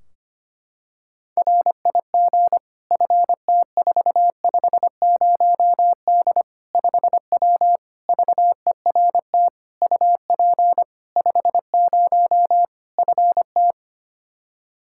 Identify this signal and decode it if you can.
Morse code, heard as a keyed tone, about 25 wpm, 700 Hz. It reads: RIG FT450D 5W VERT UP 50 FT
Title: Practice QSO - Exchange 2 - Info - 25wpm